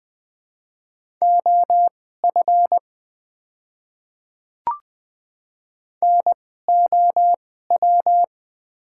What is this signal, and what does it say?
Morse code, heard as a keyed tone, about 20 wpm, spaced as for 14 wpm, 700 Hz. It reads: OF NOW
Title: 2x Character Spacing - Top 100 Words - 20wpm